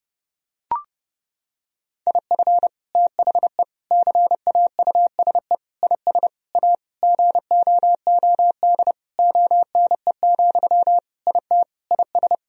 Morse code, heard as a keyed tone, about 30 wpm, 700 Hz. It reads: IF THE CAUSE IS A GOOD ONE, IT IS
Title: Sentences from Top 1000 Words - Review Part 1 - 30wpm